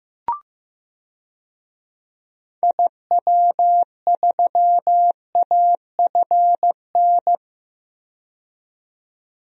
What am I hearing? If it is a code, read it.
Morse code, heard as a keyed tone, about 15 wpm, 700 Hz. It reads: IW3AFN
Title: DXCC-100 Call Signs - 6 Characters - 15wpm